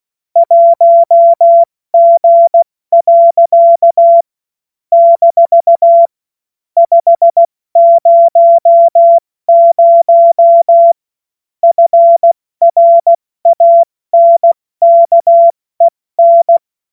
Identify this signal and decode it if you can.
Morse code, heard as a keyed tone, about 16 wpm, spaced as for 14 wpm, 680 Hz. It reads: 1G. - 500 FRANKEN
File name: Morsekurs / Witze / Witz 07. Juni 2022 70 F80